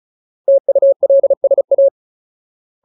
Morse code, heard as a keyed tone, about 34 wpm, 550 Hz. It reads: TULSA
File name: Ultimate Word List / 35WPM Ultimate